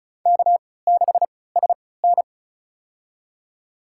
Morse code, heard as a keyed tone, about 36 wpm, spaced as for 20 wpm, 700 Hz. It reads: K6SN
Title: USA Call Signs - 4 Characters - 3x Character Spacing - 35wpm